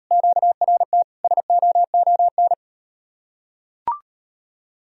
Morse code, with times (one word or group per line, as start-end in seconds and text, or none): 0.1-1.0 QRT
1.2-2.5 SOON